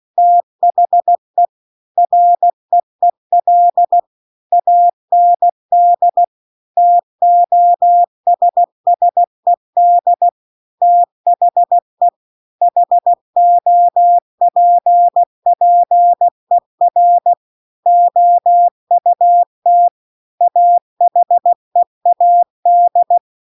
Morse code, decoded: THE REEL AND TOSSED THE HOPPER OUT AHEAD

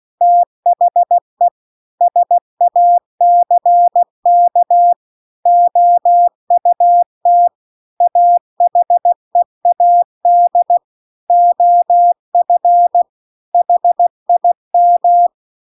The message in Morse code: THE SACK OUT AHEAD OF HIM

